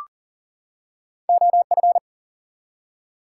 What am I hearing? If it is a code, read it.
Morse code, heard as a keyed tone, about 40 wpm, 700 Hz. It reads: OF
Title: Two Letter Words - 40wpm